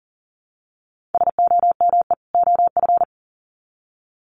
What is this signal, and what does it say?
Morse code, heard as a keyed tone, about 40 wpm, 700 Hz. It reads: SOME OF
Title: Sets of 2 Words 40wpm